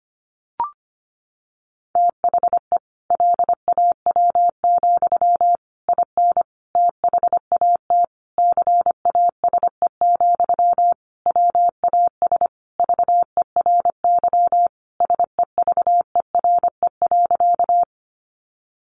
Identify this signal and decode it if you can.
Morse code, heard as a keyed tone, about 25 wpm, 700 Hz. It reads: THE LAW, IN THAT CASE, WAS VERY SEVERE.
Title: Sentences from Top 1400 Words 25wpm